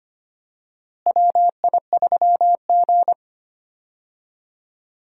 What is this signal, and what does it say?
Morse code, heard as a keyed tone, about 25 wpm, 700 Hz. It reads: WI3G
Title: USA Call Signs - 4 Characters - Repeated 3x - 25wpm